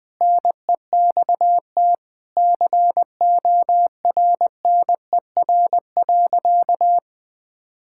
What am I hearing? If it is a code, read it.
Morse code, heard as a keyed tone, about 21 wpm, 700 Hz. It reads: NEXT CORNER.
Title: Sentences from Top 1600 Words 20wpm